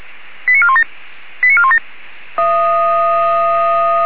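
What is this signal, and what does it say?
Five-tone selective call (ZVEI), heard as quick runs of five tones, tones of about 80 ms. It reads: 89417 89417
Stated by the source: ZVEI five-tone selective call received on SDR.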